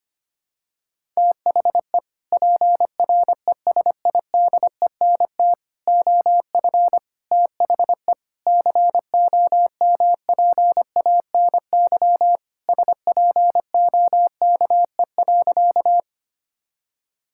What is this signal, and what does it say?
Morse code, heard as a keyed tone, about 25 wpm, 700 Hz. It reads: THE PRESIDENT OF THE COMPANY SPOKE.
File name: practice / 25wpm / Sentences with 3-Word Phrases - 25wpm